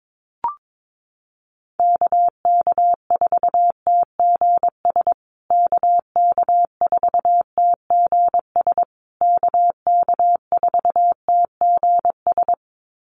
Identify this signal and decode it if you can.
Morse code, heard as a keyed tone, about 22 wpm, 700 Hz. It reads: KK4TGS KK4TGS KK4TGS
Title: USA Call Signs Repeated 3x 22wpm